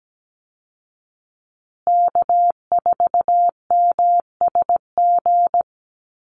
Morse code, heard as a keyed tone, about 17 wpm, 700 Hz. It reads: K4MSG
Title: Call Signs Repeated 2x - 17wpm